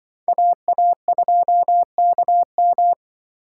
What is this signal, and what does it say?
Morse code, heard as a keyed tone, about 25 wpm, 700 Hz. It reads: AA2KM